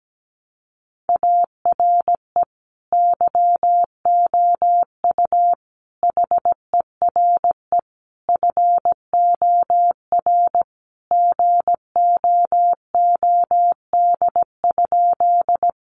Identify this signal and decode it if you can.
Morse code, heard as a keyed tone, about 17 wpm, 700 Hz. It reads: ARE YOU HERE FOR GOOD?
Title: Sentences from Top 200 Words 17wpm